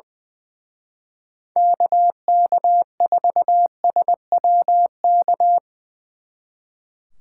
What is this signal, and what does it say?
Morse code, heard as a keyed tone, about 23 wpm, 700 Hz. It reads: KK4SWK
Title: Speed Racing - USA 2x3 Call Signs 20wpm